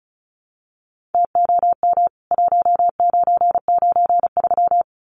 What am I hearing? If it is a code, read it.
Morse code, heard as a keyed tone, about 34 wpm, 700 Hz. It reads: TOM 1993
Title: CWT Exchange 35wpm